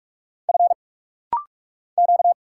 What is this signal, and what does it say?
Morse code, heard as a keyed tone, about 44 wpm, 700 Hz. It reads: F Q
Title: Single Letter-Number Alt - Rapid-Fire 45wpm